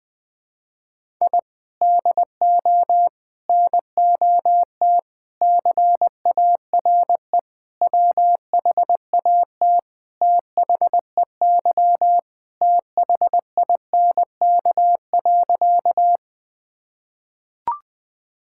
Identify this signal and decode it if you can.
Morse code, heard as a keyed tone, about 20 wpm, 700 Hz. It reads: I DO NOT CARE WHAT THEY THINK.
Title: Sentences with 5-Word Phrases - Spoken Beginning - 20wpm